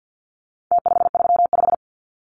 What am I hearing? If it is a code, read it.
Morse code, heard as a keyed tone, about 50 wpm, 700 Hz. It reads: T535
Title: Numbers - Contest Style - 4-Digit 50wpm